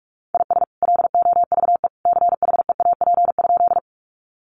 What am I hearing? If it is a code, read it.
Morse code, heard as a keyed tone, about 44 wpm, 700 Hz. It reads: IS LOVE CHEAP?
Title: Sentences from Top 1200 Words 45wpm